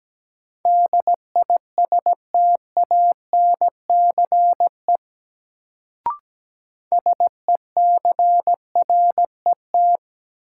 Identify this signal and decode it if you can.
Morse code, heard as a keyed tone, about 17 wpm, 700 Hz. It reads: DISTANCE SECRET
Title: Sentences from Top 1200 Words - New Word Drill - 17wpm